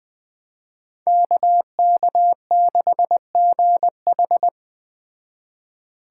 Morse code, heard as a keyed tone, about 20 wpm, 700 Hz. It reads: KK6GH
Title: Speed Racing - USA 2x2 Call Signs 17wpm